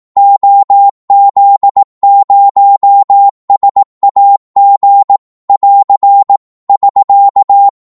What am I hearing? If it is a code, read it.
Morse code, heard as a keyed tone, about 18 wpm, 800 Hz. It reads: OZ0SAG <AR> <SK>
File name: Morse / qso-18wpm